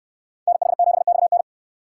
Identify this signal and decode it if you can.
Morse code, heard as a keyed tone, about 67 wpm, 700 Hz. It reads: NH7XN